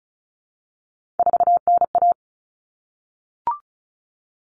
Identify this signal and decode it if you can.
Morse code, heard as a keyed tone, about 34 wpm, 700 Hz. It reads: 4NA
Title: Common Cut Numbers - 3-Digit 35wpm